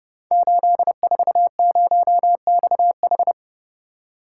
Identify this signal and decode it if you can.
Morse code, heard as a keyed tone, about 30 wpm, 700 Hz. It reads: 840XH
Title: Code Groups - 5 Letter:Numbers - 30wpm